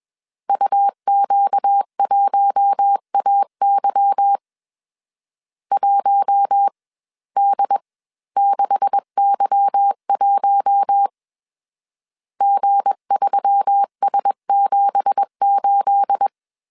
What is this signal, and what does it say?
Morse code, heard as a keyed tone, about 21 wpm, spaced as for 18 wpm, 800 Hz. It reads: UQ1AY 1D6Y1 G3S78
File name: CW / DET / 1 9 1500C S C 20wpm 2310